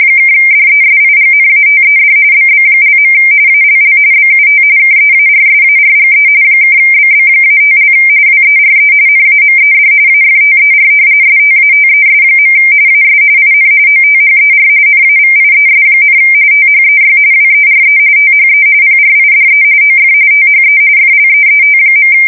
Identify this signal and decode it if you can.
RTTY, about 45.45 baud. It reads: (DESIGNATED BB). ALL BUT KEARSARGE (BB-5) WERE NAMED AFTER STATES. AS IN OTHER NAVIES, THEY WERE THE PRIDE OF THE FLEET.
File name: fsk170 / battleshipnj